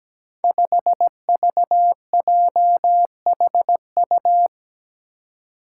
Morse code, heard as a keyed tone, about 17 wpm, 700 Hz. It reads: 5VJHU
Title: Five Challenging Letter-Number Combinations 17wpm